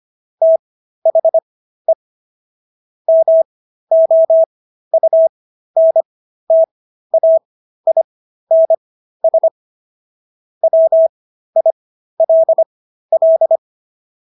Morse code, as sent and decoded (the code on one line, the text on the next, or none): - .... / -- --- ..- -. - .- .. -. ... / .-- .. .-.. .-..
THE MOUNTAINS WILL